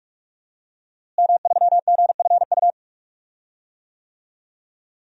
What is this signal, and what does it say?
Morse code, heard as a keyed tone, about 41 wpm, 700 Hz. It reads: M3GFU
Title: DXCC-100 Call Signs - 5 Characters - Repeated 2x - 45wpm